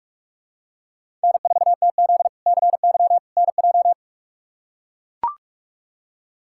Morse code, heard as a keyed tone, about 43 wpm, 700 Hz. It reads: N4TZ CY NJ
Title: K1USN SST QSO Practice - 45wpm